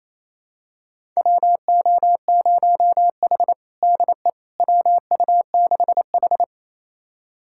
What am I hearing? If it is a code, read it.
Morse code, heard as a keyed tone, about 28 wpm, 700 Hz. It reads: WO0H DE WU6H
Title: USA Callsign DE Callsign - 4 Characters - 28wpm